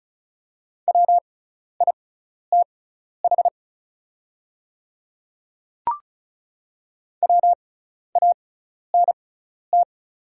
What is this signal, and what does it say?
Morse code, heard as a keyed tone, about 36 wpm, spaced as for 12 wpm, 690 Hz. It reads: WITH WANT
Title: 6x Character Spacing - Top 200 Words - 35wpm